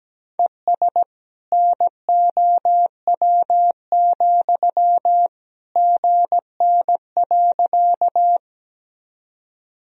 Morse code, heard as a keyed tone, about 17 wpm, 700 Hz. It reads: ES NOW, GN.
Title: Sentences from Top 300 Words and Abbreviations 17wpm